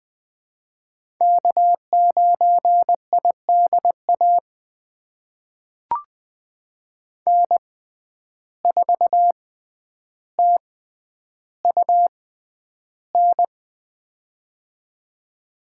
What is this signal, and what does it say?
Morse code, heard as a keyed tone, about 21 wpm, spaced as for 10 wpm, 700 Hz. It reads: K9IDA N4TUN
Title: USA Call Signs - 6x Character Spacing - 20wpm